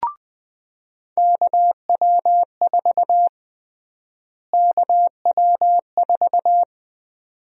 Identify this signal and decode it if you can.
Morse code, heard as a keyed tone, about 21 wpm, 700 Hz. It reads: KW4 KW4
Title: USA Call Sign Prefixes - Repeated 2x - 20wpm